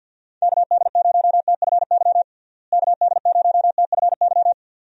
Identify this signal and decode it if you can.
Morse code, heard as a keyed tone, about 50 wpm, 700 Hz. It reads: KD0TFY KD0TFY
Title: USA Call Signs - 6 Characters - Repeated 2x - 50wpm